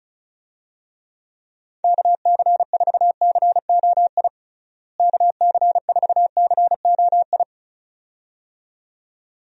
Morse code, heard as a keyed tone, about 36 wpm, 700 Hz. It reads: KC4COI KC4COI
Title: USA Call Signs - 6 Characters - Repeated 2x - 35wpm